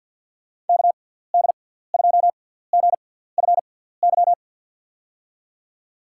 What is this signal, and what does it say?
Morse code, heard as a keyed tone, about 48 wpm, spaced as for 17 wpm, 700 Hz. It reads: KD2GFY
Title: USA Call Signs - 6 Characters - 6x Character Spacing - 50wpm